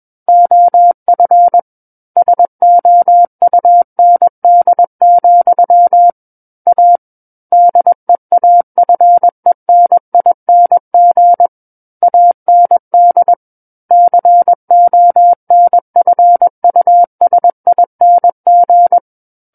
Morse code, heard as a keyed tone, about 21 wpm, 700 Hz. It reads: OF SOUND, A DEAFENING AND CONFUSING
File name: Book One / 12-Chapter 12 - 21 WPM